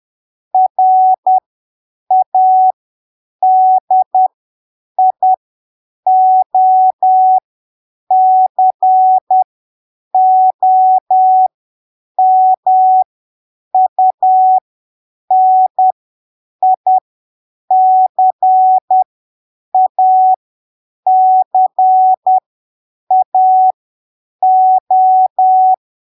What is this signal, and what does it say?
Morse code, heard as a keyed tone, about 10 wpm, spaced as for 7 wpm, 750 Hz. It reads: RADIOCOMUNICACAO
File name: arquivos / texto / texto 01 10ppm